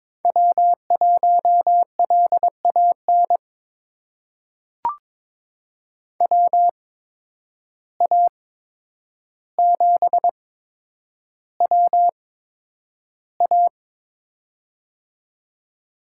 Morse code, heard as a keyed tone, about 22 wpm, spaced as for 9 wpm, 700 Hz. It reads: W1LAN WA7WA
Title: USA Call Signs - 5 Characters - 8x Character Spacing - 22wpm